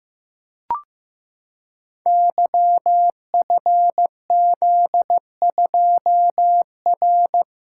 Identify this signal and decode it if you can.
Morse code, heard as a keyed tone, about 15 wpm, 700 Hz. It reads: YFZ2R
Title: Code Groups - 5 Letter:Numbers - 15wpm